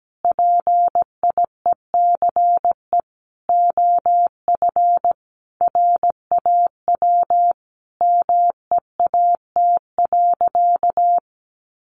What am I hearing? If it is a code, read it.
Morse code, heard as a keyed tone, about 17 wpm, 700 Hz. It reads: PIECE OF RAW MEAT.
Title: Sentences from Top 1500 Words 17wpm